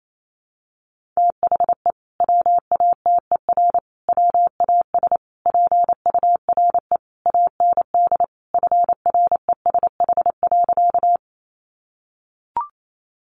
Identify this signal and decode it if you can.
Morse code, heard as a keyed tone, about 28 wpm, 700 Hz. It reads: THE WATER WAS PURE AND FRESH.
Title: Sentences from Top 1500 Words 28wpm